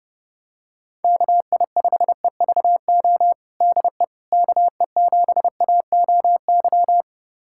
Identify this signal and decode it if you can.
Morse code, heard as a keyed tone, about 30 wpm, 700 Hz. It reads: KI5EVO DE KE7AOY
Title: USA Callsign DE Callsign 30wpm